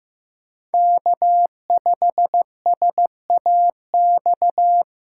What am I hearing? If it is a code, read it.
Morse code, heard as a keyed tone, about 15 wpm, 700 Hz. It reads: K5SAX
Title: USA Call Signs - 3x Character Spacing - 15wpm